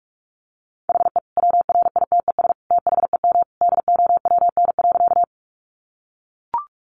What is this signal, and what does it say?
Morse code, heard as a keyed tone, about 44 wpm, 700 Hz. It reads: HE WRITES THEM DOWN.
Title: Additional Easy Phrases 45wpm